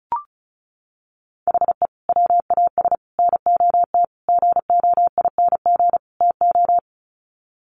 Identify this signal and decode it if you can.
Morse code, heard as a keyed tone, about 34 wpm, 700 Hz. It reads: HE WAS NOT GOING TO
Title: Sets of 5 Words 35wpm